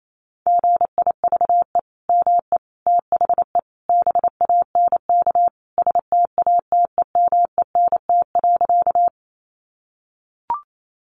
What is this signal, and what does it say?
Morse code, heard as a keyed tone, about 28 wpm, 700 Hz. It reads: GIVE ME THE BANK STATEMENT.
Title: Sentences from Top 1200 Words 28wpm